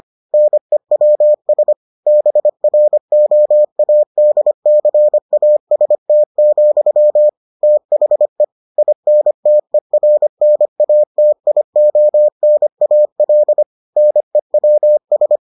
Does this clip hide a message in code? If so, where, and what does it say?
Yes: Morse code from 0.3 to 15.4 s, heard as a keyed tone, about 25 wpm, 600 Hz. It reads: NEWS BROADCAST, THE INTERNATIONAL NEWS